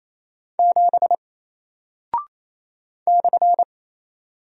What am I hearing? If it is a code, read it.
Morse code, heard as a keyed tone, about 27 wpm, 700 Hz. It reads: 7 /
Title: Single Character (Letter:Number:Punctuation:Prosign) 28wpm